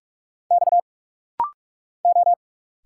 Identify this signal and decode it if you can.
Morse code, heard as a keyed tone, about 44 wpm, 700 Hz. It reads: X O
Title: Single Letter-Number Alt - Rapid-Fire 45wpm